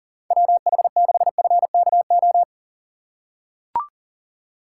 Morse code, heard as a keyed tone, about 37 wpm, 700 Hz. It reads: WH6FKO
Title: USA Call Signs - 6 Characters - 2x Character Spacing - 40wpm